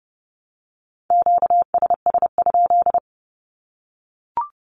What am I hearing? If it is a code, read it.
Morse code, heard as a keyed tone, about 30 wpm, 700 Hz. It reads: QSS?
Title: Q-Codes 30wpm